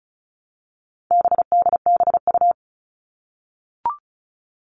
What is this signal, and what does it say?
Morse code, heard as a keyed tone, about 34 wpm, 700 Hz. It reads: BDBU
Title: All Cut Numbers - 4-Digit 35wpm